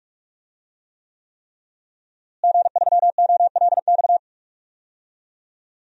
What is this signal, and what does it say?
Morse code, heard as a keyed tone, about 46 wpm, 700 Hz. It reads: G3OLX
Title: DXCC-100 Call Signs - 5 Characters - Repeated 3x - 45wpm